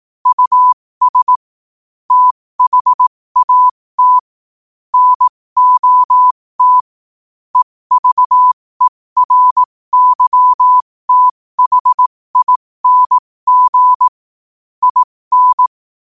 Morse code, heard as a keyed tone, about 18 wpm, spaced as for 15 wpm, 1000 Hz. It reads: US THAT NOT EVERYTHING IN
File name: qotdfiles / 15wpm